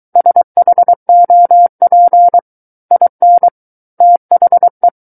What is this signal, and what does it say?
Morse code, heard as a keyed tone, about 23 wpm, 700 Hz. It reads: SHOP IN THE
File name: Book One / 14-Chapter 14 - 23 WPM